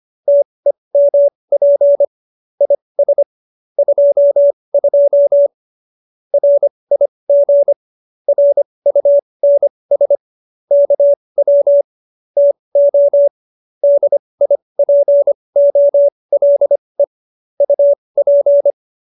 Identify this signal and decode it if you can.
Morse code, heard as a keyed tone, about 25 wpm, spaced as for 20 wpm, 570 Hz. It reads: TEMP IS 22 RIG RUNS KW TO DIPOLE UP
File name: qso220b 20